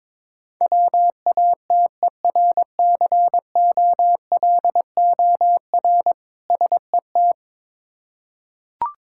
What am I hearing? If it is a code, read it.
Morse code, heard as a keyed tone, about 22 wpm, 690 Hz. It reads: WATERCOLOR SET
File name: Difficult Words with Hints - 22wpm